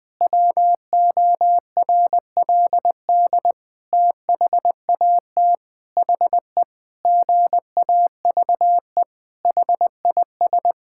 Morse code, heard as a keyed tone, about 20 wpm, 700 Hz. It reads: WORLD THAT HE GAVE HIS
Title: Sentences with 8-Word Phrases - 20wpm